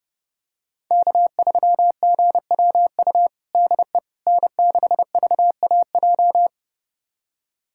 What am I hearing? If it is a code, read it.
Morse code, heard as a keyed tone, about 30 wpm, 700 Hz. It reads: K3GWU DE N6VAJ
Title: USA Callsign DE Callsign - 5 Characters - 30wpm